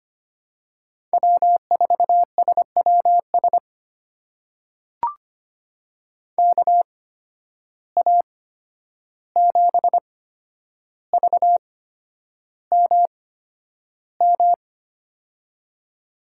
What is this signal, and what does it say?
Morse code, heard as a keyed tone, about 25 wpm, spaced as for 10 wpm, 700 Hz. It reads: W4SWS KA7VMM
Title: USA Call Signs - 8x Character Spacing - 25wpm